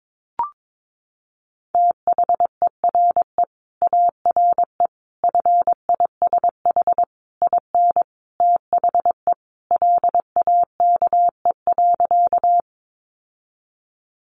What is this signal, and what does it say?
Morse code, heard as a keyed tone, about 22 wpm, 700 Hz. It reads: THERE ARE FISH IN THE LAKE.
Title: Sentences from Top 1100 Words 22wpm